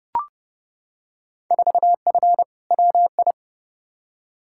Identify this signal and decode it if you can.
Morse code, heard as a keyed tone, about 30 wpm, 700 Hz. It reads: 4F WI
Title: Field Day Exchanges Encore - 30wpm